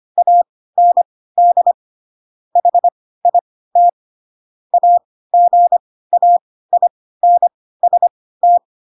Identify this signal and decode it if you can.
Morse code, heard as a keyed tone, about 25 wpm, spaced as for 16 wpm, 700 Hz. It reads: AND HIT AGAINST
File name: Pinocchio / Pinocchio ch2 10 25 ik1hge